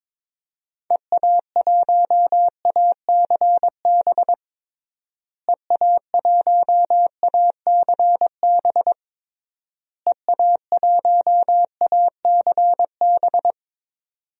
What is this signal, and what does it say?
Morse code, heard as a keyed tone, about 22 wpm, 700 Hz. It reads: EA1ACB EA1ACB EA1ACB
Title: DXCC-100 Call Signs - 6 Characters - Repeated 3x - 22wpm